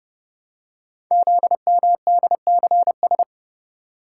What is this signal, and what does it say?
Morse code, heard as a keyed tone, about 30 wpm, 700 Hz. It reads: ZMDCS
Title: Code Groups - 5 Letters - 30wpm